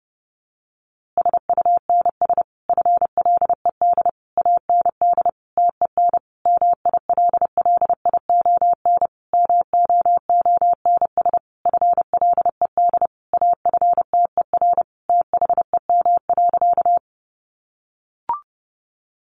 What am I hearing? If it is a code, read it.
Morse code, heard as a keyed tone, about 28 wpm, 700 Hz. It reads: SUNS FLED AND TEN MILLION MOONS FLED AFTER THEM.
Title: A Sound Of Thunder 30wpm